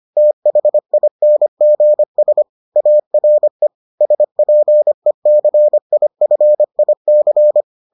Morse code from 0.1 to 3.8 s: THINGS ARE

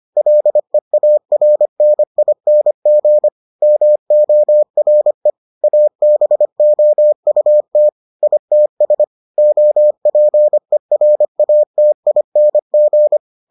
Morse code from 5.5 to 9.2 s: ABOUT ITS

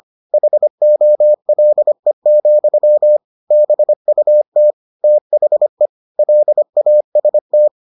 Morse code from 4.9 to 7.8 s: THE LAST

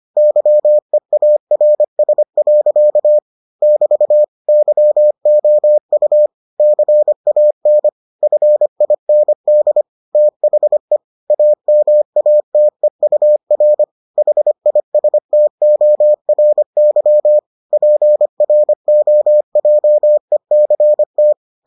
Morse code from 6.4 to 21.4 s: CAN FIND THE AMATEUR HISTORY PROJECT

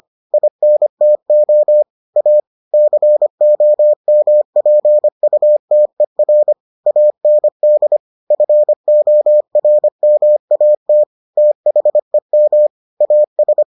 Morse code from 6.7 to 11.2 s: AND FORMAT

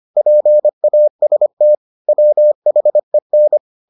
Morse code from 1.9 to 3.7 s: WHEN

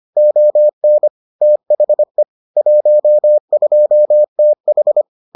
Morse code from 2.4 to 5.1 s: 12TH

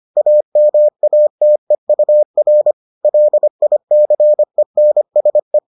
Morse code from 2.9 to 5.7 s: LICENSE

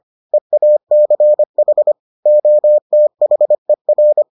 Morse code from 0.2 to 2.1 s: EACH

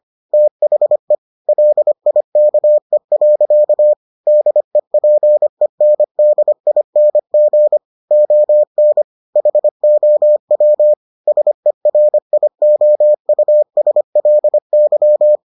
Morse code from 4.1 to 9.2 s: DEPENDING ON

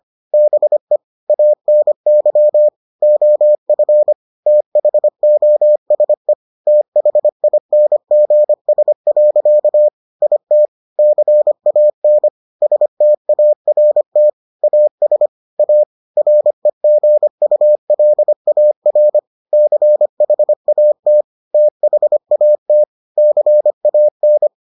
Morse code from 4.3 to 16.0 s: THOSE THINGS. IT CAN START AS A

